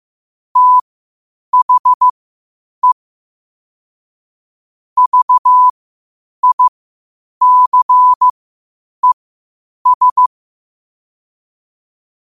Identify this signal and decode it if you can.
Morse code, heard as a keyed tone, about 15 wpm, spaced as for 8 wpm, 1000 Hz. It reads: THE VICES
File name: qotdfiles / 8wpm